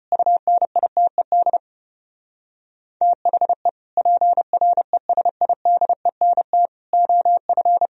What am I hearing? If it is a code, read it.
Morse code, heard as a keyed tone, about 33 wpm, 700 Hz. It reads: UNITED THE PRESIDENT OF